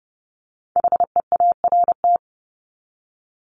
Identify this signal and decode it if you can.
Morse code, heard as a keyed tone, about 30 wpm, 700 Hz. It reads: HEART
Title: Top 1001-1250 Words 30wpm